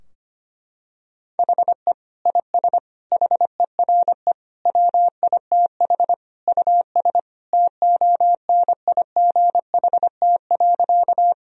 Morse code, heard as a keyed tone, about 25 wpm, 700 Hz. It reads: HE IS HERE WITH US TONIGHT.
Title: Sentences from Top 1000 Words - Review Part 3 - 25wpm